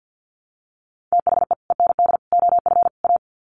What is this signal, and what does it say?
Morse code, heard as a keyed tone, about 48 wpm, 700 Hz. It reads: THE END OF A
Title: Sets of 4 Words 50wpm